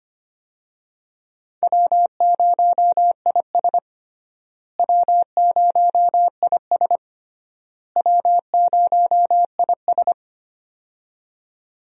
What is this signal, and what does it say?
Morse code, heard as a keyed tone, about 25 wpm, 700 Hz. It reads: W0IS W0IS W0IS
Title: USA Call Signs - 4 Characters - Repeated 3x - 25wpm